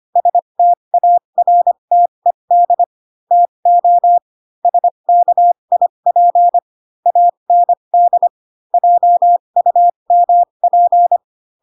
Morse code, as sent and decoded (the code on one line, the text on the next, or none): ... - .- .-. - . -.. / - --- / ... -.- .. .--. / .- -. -.. / .--- ..- -- .--.
STARTED TO SKIP AND JUMP